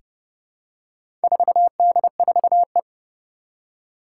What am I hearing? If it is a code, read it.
Morse code, heard as a keyed tone, about 30 wpm, 700 Hz. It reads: 4D4E
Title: Four Challenging Letter-Number Combinations 30wpm